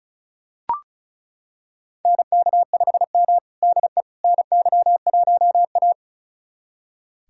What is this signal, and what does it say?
Morse code, heard as a keyed tone, about 34 wpm, 700 Hz. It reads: NK5M DE NY1A
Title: USA Callsign DE Callsign - 4 Characters - 35wpm